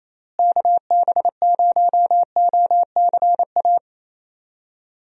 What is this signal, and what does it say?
Morse code, heard as a keyed tone, about 28 wpm, 700 Hz. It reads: KB0OCA
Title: USA Call Signs - 6 Characters - Repeated 2x - 28wpm